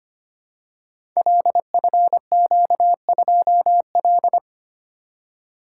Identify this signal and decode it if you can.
Morse code, heard as a keyed tone, about 25 wpm, 690 Hz. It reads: LFQ2L
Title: Code Groups - 5 Letter:Numbers - 25wpm